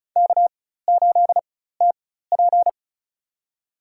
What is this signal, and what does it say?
Morse code, heard as a keyed tone, about 36 wpm, spaced as for 16 wpm, 700 Hz. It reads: K8TP
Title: USA Call Signs - 4x Character Spacing - 35wpm